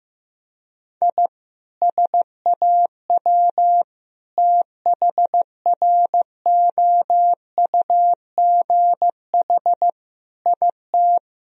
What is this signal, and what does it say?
Morse code, heard as a keyed tone, about 15 wpm, 700 Hz. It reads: I SAW THROUGH IT
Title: Sentences from Top 1000 Words - Review Part 1 - 15wpm